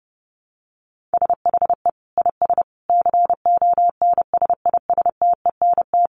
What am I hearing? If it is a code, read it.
Morse code, heard as a keyed tone, about 30 wpm, 700 Hz. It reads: SHE IS CONSISTENT
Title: Sentences from Top 1500 Words 30wpm